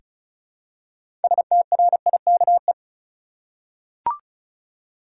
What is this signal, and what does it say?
Morse code, heard as a keyed tone, about 34 wpm, 700 Hz. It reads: STRIKE
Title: Six Letter Words - 35wpm